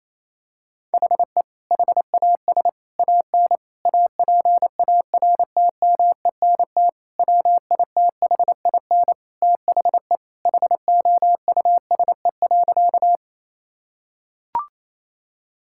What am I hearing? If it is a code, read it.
Morse code, heard as a keyed tone, about 28 wpm, 700 Hz. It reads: HE HAS AN APARTMENT WITHIN THE HOUSE.
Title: Sentences from Top 1000 Words - Review Part 6 - 28wpm